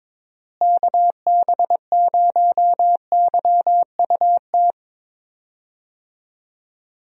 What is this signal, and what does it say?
Morse code, heard as a keyed tone, about 22 wpm, 700 Hz. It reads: KB0YUT
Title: USA Call Signs - 6 Characters - Repeated 2x - 22wpm